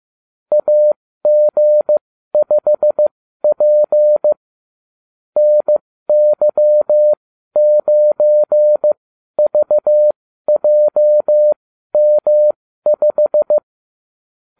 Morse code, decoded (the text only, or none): AG5P NY9VJM5